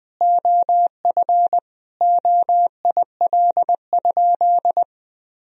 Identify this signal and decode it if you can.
Morse code, heard as a keyed tone, about 20 wpm, 700 Hz. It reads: OF OIL?
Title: Sentences from Top 1000 Words - Review Part 4 - 20wpm